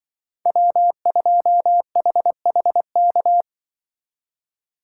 Morse code, heard as a keyed tone, about 25 wpm, 700 Hz. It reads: W2HHK
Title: Speed Racing - USA 1x3 Call Signs 20wpm